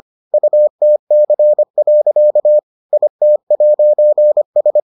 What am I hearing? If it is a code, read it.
Morse code, heard as a keyed tone, about 25 wpm, 600 Hz. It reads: UTC. IT'S